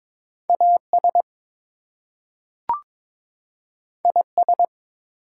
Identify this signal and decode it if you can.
Morse code, heard as a keyed tone, about 22 wpm, 700 Hz. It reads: AS IS
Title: Two Letter Words - 22wpm